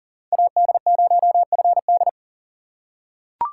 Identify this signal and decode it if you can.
Morse code, heard as a keyed tone, about 40 wpm, 690 Hz. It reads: AD0FD
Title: USA Call Signs - 5 Characters - 3x Character Spacing - 40wpm